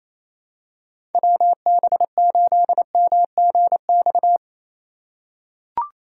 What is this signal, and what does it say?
Morse code, heard as a keyed tone, about 28 wpm, 700 Hz. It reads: WB8MGX
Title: USA Call Signs - 4x Character Spacing - 28wpm